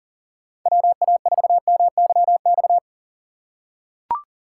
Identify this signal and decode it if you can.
Morse code, heard as a keyed tone, about 40 wpm, 700 Hz. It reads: WA4MYX
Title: USA Call Signs - 2x Character Spacing - 40wpm